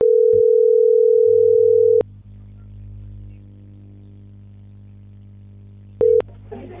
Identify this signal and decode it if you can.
Ringback tone (North America), 440 + 480 Hz.